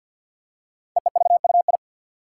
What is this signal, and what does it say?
Morse code, heard as a keyed tone, about 48 wpm, 700 Hz. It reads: EEVUI
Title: Code Groups - 5 Letters - 50wpm